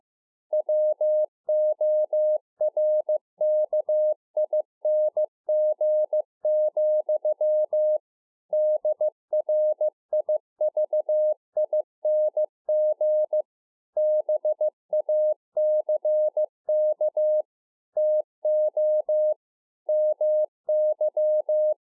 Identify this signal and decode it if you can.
Morse code, heard as a keyed tone, about 15 wpm, 610 Hz. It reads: WORKING, DRIVING BACK TO MY
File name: 15WPM